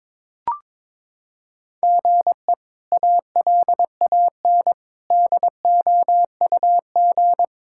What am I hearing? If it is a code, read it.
Morse code, heard as a keyed tone, about 21 wpm, 700 Hz. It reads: GE ALAN DOUG
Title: K1USN SST QSO Practice - 22wpm